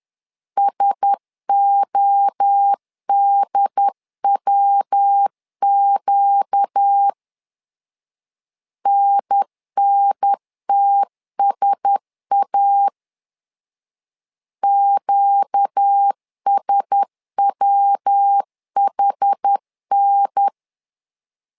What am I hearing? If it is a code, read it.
Morse code, heard as a keyed tone, about 10 wpm, 800 Hz. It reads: SODWQ NNTSA QSWHN
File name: LEZIONI / 4 Q1S 10wpm 2133